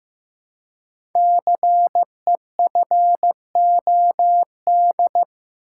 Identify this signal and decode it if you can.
Morse code, heard as a keyed tone, about 15 wpm, 700 Hz. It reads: CEFOD
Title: Code Groups - 5 Letters - 15wpm